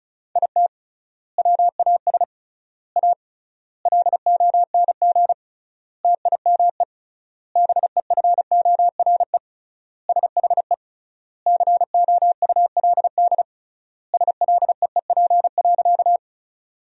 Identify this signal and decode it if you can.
Morse code, heard as a keyed tone, about 36 wpm, 700 Hz. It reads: IT WAS A LONG TIME BEFORE SHE COULD SLEEP.
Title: 3x Word-Spacing - Sentences from Top 500 Words - 35wpm